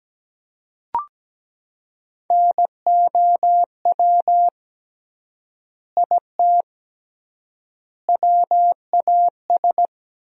Morse code, heard as a keyed tone, about 17 wpm, 700 Hz. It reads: NOW IT WAS
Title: 3x Word-Spacing - Sentences from Top 200 Words - 17wpm